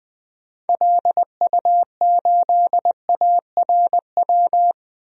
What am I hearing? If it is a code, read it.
Morse code, heard as a keyed tone, about 21 wpm, 700 Hz. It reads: LU8ARW
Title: DXCC-100 Call Signs - 6 Characters - 20wpm